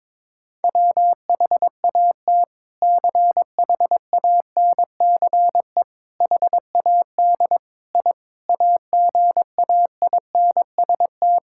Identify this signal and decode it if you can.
Morse code, heard as a keyed tone, about 22 wpm, 700 Hz. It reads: WHAT CHANCE HAD I AGAINST
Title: Sentences from Top 1000 Words - Review Part 6 - 22wpm